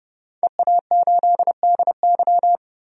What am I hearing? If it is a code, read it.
Morse code, heard as a keyed tone, about 30 wpm, 700 Hz. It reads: EA8DY
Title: DXCC-100 Call Signs - 5 Characters - 30wpm